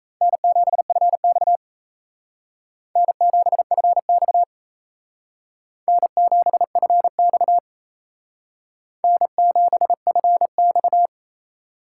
Morse code, heard as a keyed tone, about 35 wpm, 700 Hz. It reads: N7FX N7FX N7FX N7FX
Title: Speed Racing - USA Call Signs 28wpm